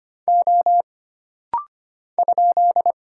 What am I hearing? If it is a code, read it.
Morse code, heard as a keyed tone, about 25 wpm, 700 Hz. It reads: O ?
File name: Single Character (Letter:Number:Punctuation:Prosign) - Rapid-Fire - 25wpm